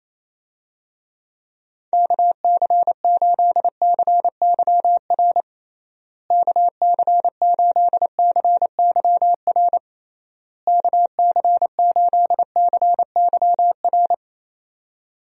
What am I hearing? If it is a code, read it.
Morse code, heard as a keyed tone, about 28 wpm, 700 Hz. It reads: KC8CYR KC8CYR KC8CYR
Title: USA Call Signs - 6 Characters - Repeated 3x - 28wpm